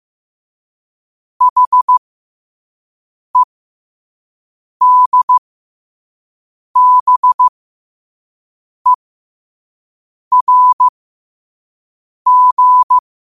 Morse code, heard as a keyed tone, about 15 wpm, spaced as for 5 wpm, 1000 Hz. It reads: HEDBERG
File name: qotdfiles / 5wpm